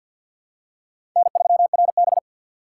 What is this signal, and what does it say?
Morse code, heard as a keyed tone, about 52 wpm, 700 Hz. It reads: N3RB
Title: USA Call Signs - 4 Characters - 6x Character Spacing - 50wpm